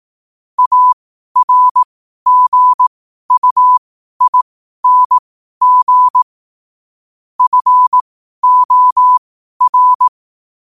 Morse code, heard as a keyed tone, about 18 wpm, spaced as for 12 wpm, 1000 Hz. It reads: ARGUING FOR
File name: qotdfiles / 12wpm